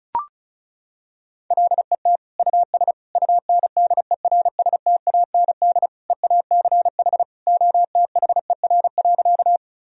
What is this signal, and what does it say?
Morse code, heard as a keyed tone, about 34 wpm, 700 Hz. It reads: LET US UNDERSTAND EACH OTHER.
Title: Sentences from Top 1000 Words - Review Part 1 - 35wpm